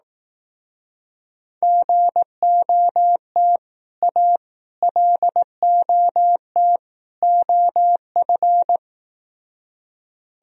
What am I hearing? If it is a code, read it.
Morse code, heard as a keyed tone, about 18 wpm, 700 Hz. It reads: GOT A LOT OF
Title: Speed-Racing - Sets of 4 Words 15wpm